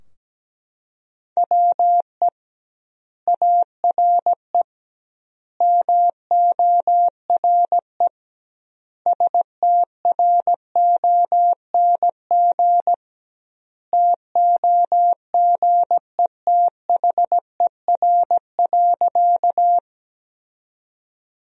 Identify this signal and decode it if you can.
Morse code, heard as a keyed tone, about 17 wpm, 700 Hz. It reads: WE ARE MORE STRONG TOGETHER.
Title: 2x Word-Spacing - Sentences from Top 500 Words - 17wpm